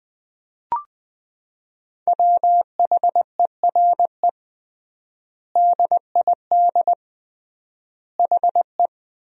Morse code, heard as a keyed tone, about 20 wpm, 700 Hz. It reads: WHERE DID HE
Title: 3x Word-Spacing - Sentences from Top 200 Words - 20wpm